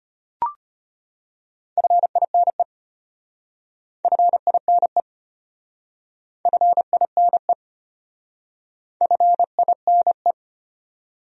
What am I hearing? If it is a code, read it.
Morse code, heard as a keyed tone, about 33 wpm, 700 Hz. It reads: FINE FINE FINE FINE